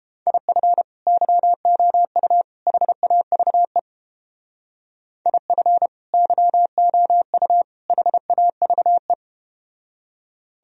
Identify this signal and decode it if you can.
Morse code, heard as a keyed tone, about 31 wpm, 700 Hz. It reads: IF YOU HAVE IF YOU HAVE